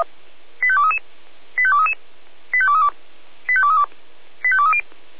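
Five-tone selective call (ZVEI), heard as quick runs of five tones, tones of about 70 ms. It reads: 85233 85233 85232 85232 85230
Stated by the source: ZVEI five-tone selective call 85852 received by SDR, from the feuerwehr-orlamuende audio server.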